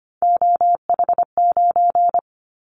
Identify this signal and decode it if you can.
Morse code, heard as a keyed tone, about 25 wpm, 700 Hz. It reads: OH9
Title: Callsign Prefixes for Top 50 DXCC - 25wpm